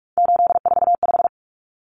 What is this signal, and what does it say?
Morse code, heard as a keyed tone, about 44 wpm, 700 Hz. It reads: 845
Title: Common Cut Numbers - 3-Digit 45wpm